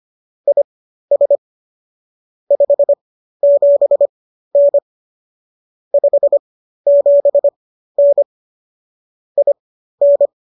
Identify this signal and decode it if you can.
Morse code, heard as a keyed tone, about 25 wpm, spaced as for 13 wpm, 570 Hz. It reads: IS 57N 57N IN